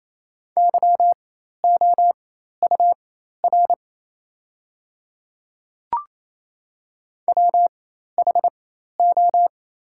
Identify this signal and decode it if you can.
Morse code, heard as a keyed tone, about 28 wpm, spaced as for 13 wpm, 700 Hz. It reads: YOUR WHO
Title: 4x Character Spacing - Top 100 Words - 28wpm